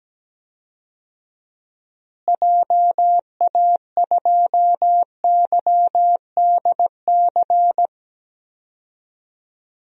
Morse code, heard as a keyed tone, about 17 wpm, 700 Hz. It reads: JA2YDC
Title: DXCC-100 Call Signs - 6 Characters - Repeated 3x - 17wpm